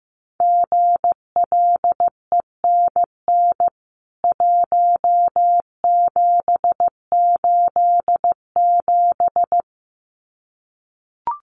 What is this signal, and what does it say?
Morse code, heard as a keyed tone, about 15 wpm, 700 Hz. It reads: GLENN 1787
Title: CWT Exchange 15wpm